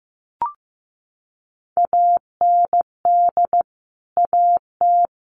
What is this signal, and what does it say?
Morse code, heard as a keyed tone, about 15 wpm, 700 Hz. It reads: AND AT
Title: Sentences from Top 1200 Words 15wpm